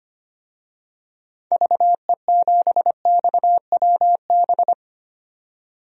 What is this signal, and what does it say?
Morse code, heard as a keyed tone, about 25 wpm, 700 Hz. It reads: VE7XWB
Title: DXCC-100 Call Signs - 6 Characters - 25wpm